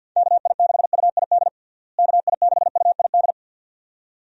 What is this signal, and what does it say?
Morse code, heard as a keyed tone, about 48 wpm, 700 Hz. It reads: KI6UID KI6UID
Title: USA Call Signs - 6 Characters - Repeated 2x - 50wpm